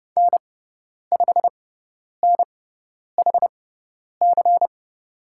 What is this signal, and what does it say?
Morse code, heard as a keyed tone, about 30 wpm, spaced as for 10 wpm, 700 Hz. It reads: N5NHC
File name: Lesson 18g - Letter m - Call Signs 30wpm